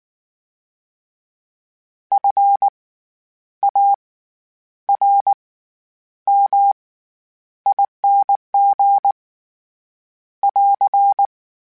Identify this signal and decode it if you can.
Morse code, heard as a keyed tone, about 19 wpm, spaced as for 9 wpm, 800 Hz. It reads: FARMING <AR>
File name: Special INGs / Special INGs-mp3s / SlowToFaster ing @18